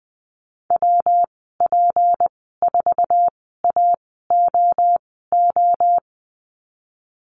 Morse code, heard as a keyed tone, about 20 wpm, spaced as for 15 wpm, 700 Hz. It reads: WP4AOO
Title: 154 Lesson 25g - Letter p - Call Signs 20wpm